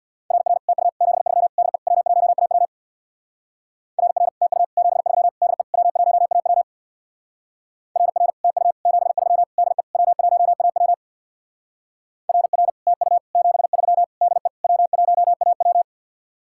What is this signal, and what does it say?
Morse code, heard as a keyed tone, about 60 wpm, 700 Hz. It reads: RR TU 73 DE W1AW RR TU 73 DE W1AW RR TU 73 DE W1AW RR TU 73 DE W1AW
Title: Speed-Racing Common QSO Phrases - 50wpm